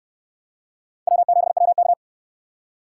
Morse code, heard as a keyed tone, about 67 wpm, 700 Hz. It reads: W7WK